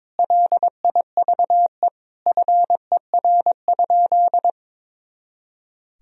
Morse code, heard as a keyed tone, about 22 wpm, 700 Hz. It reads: LIVE FER?
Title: Sentences from Top 200 Words and Abbreviations 22wpm